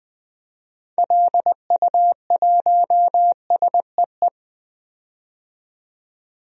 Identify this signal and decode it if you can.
Morse code, heard as a keyed tone, about 20 wpm, 700 Hz. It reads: LU1SEE
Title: DXCC-100 Call Signs - 6 Characters - 20wpm